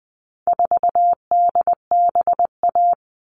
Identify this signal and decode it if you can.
Morse code, heard as a keyed tone, about 20 wpm, 700 Hz. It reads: 4DBA
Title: All Cut Numbers - 4-Digit 20wpm